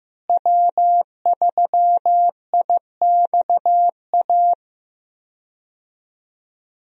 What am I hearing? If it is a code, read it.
Morse code, heard as a keyed tone, about 15 wpm, 700 Hz. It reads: W3IXA
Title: USA Call Signs - 5 Characters - Repeated 3x - 15wpm